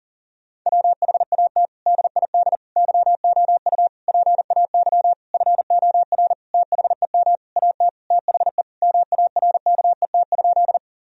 Morse code, heard as a keyed tone, about 40 wpm, 700 Hz. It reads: WHAT DID YOU PAY FOR THEM AT THE MARKET?
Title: Sentences from Top 1000 Words - Review Part 3 - 40wpm